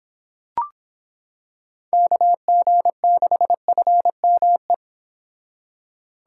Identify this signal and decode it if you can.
Morse code, heard as a keyed tone, about 26 wpm, 700 Hz. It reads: KG6FME